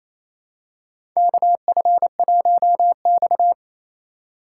Morse code, heard as a keyed tone, about 28 wpm, 700 Hz. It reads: KF1X
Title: USA Call Signs - 4 Characters - 2x Character Spacing - 28wpm